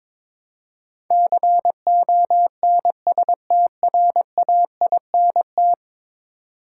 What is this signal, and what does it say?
Morse code, heard as a keyed tone, about 21 wpm, 700 Hz. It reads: CONSTRAINT
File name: practice / 22wpm / 10 Letter Words - First 3 Letters Given - 22wpm